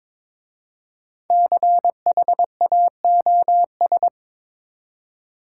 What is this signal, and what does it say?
Morse code, heard as a keyed tone, about 22 wpm, 700 Hz. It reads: CHAOS
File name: 5 Letter Words - First 3 Letters Given - 22wpm